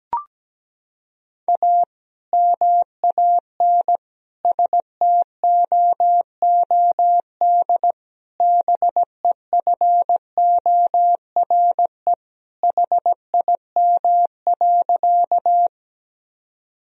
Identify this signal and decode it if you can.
Morse code, heard as a keyed tone, about 17 wpm, 700 Hz. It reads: A MAN STOOD BEFORE HIM.
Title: Sentences from Top 1000 Words - Review Part 2 - 17wpm